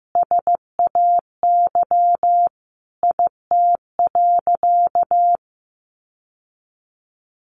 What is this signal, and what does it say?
Morse code, heard as a keyed tone, about 15 wpm, 700 Hz. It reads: SAY IT.
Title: Sentences from Top 1000 Words 15wpm